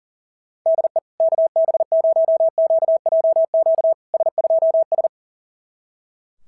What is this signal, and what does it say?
Morse code, heard as a keyed tone, about 37 wpm, 650 Hz. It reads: DE KB0QJQ S2S
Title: SOTA QSO Practice 40wpm